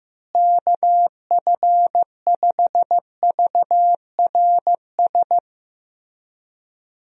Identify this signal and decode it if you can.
Morse code, heard as a keyed tone, about 15 wpm, 700 Hz. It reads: KF5VRS
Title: USA Call Signs - 6 Characters - Repeated 2x - 15wpm